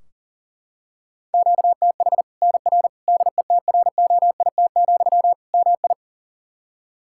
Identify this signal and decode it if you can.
Morse code, heard as a keyed tone, about 40 wpm, 700 Hz. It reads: QTH NR DETROIT, MI
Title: K1USN SST QSO Practice 40wpm